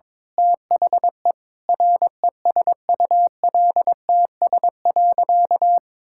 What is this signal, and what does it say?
Morse code, heard as a keyed tone, about 22 wpm, 700 Hz. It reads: THE RESULTS.
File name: Sentences with 7-Word Phrases - 22wpm